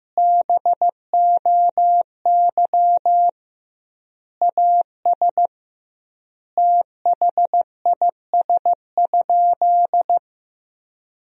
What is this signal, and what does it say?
Morse code, heard as a keyed tone, about 15 wpm, 700 Hz. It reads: BOY AS THIS?
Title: 2x Word-Spacing - Sentences from Top 400 Words - 15wpm